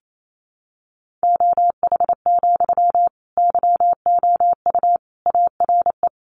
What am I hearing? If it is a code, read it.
Morse code, heard as a keyed tone, about 28 wpm, 700 Hz. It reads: OH, YOU ARE
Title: Sentences from Top 500 Words 28wpm